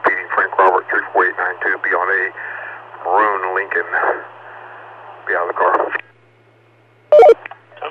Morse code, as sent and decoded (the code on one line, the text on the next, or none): none